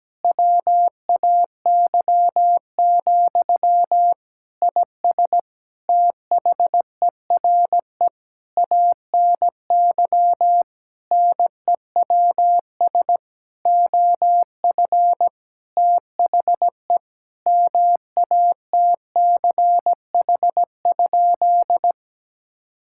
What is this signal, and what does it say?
Morse code, heard as a keyed tone, about 17 wpm, 700 Hz. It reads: WAY, IS THERE ANY NEWS OF THE MATCH?
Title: Sentences from Top 1600 Words 17wpm